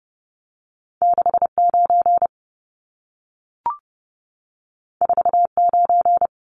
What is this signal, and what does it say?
Morse code, heard as a keyed tone, about 30 wpm, 700 Hz. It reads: 69 49
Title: Numbers - 2-Digit 30wpm